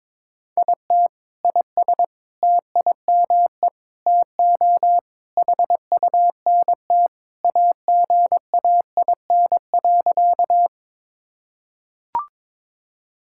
Morse code, heard as a keyed tone, about 22 wpm, 700 Hz. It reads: IT IS TIME TO HUNT AGAIN.
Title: Sentences from Top 1600 Words 22wpm